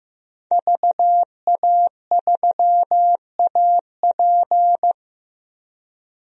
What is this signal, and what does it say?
Morse code, heard as a keyed tone, about 15 wpm, 700 Hz. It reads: VA3AP
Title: DXCC-100 Call Signs - 5 Characters - 15wpm